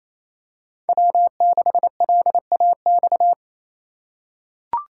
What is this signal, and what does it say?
Morse code, heard as a keyed tone, about 28 wpm, 700 Hz. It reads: W6LAX
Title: K1USN SST QSO Practice - 28wpm